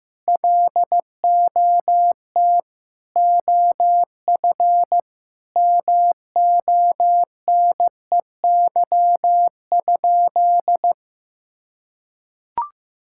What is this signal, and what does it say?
Morse code, heard as a keyed tone, about 15 wpm, 700 Hz. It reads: LOT OF MONEY?
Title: Sentences from Top 1000 Words - Review Part 5 - 15wpm